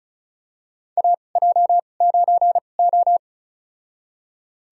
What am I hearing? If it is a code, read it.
Morse code, heard as a keyed tone, about 36 wpm, spaced as for 25 wpm, 700 Hz. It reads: AJ9O
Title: USA Call Signs - 4 Characters - 2x Character Spacing - 35wpm